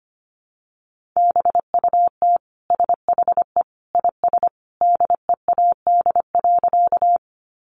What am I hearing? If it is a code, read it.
Morse code, heard as a keyed tone, about 25 wpm, 700 Hz. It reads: BUT SHE IS DEAD.
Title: Sentences from Top 1500 Words 25wpm